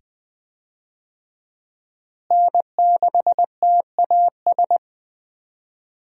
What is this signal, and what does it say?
Morse code, heard as a keyed tone, about 20 wpm, 700 Hz. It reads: N6TAS
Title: USA Call Signs - 5 Characters - Repeated 2x - 20wpm